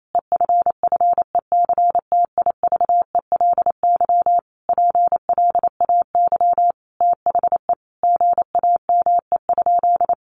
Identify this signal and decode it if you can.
Morse code, heard as a keyed tone, about 28 wpm, 700 Hz. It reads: EFFECTIVELY PLAY THE GAME?
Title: Sentences from Top 1400 Words 28wpm